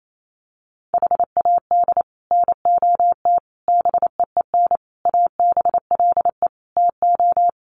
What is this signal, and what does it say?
Morse code, heard as a keyed tone, about 28 wpm, 700 Hz. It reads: HAD NOT BEEN ABLE TO
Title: Sets of 5 Words Encore 28wpm